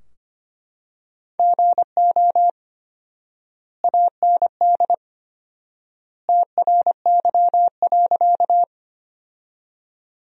Morse code, heard as a keyed tone, about 25 wpm, 700 Hz. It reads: GO AND TRY.
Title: 4x Word-Spacing - Sentences from Top 200 Words - 25wpm